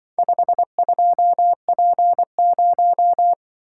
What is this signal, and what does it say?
Morse code, heard as a keyed tone, about 24 wpm, 700 Hz. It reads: 52P0
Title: Speed-Racing - Four Letter-Number Combinations - 20wpm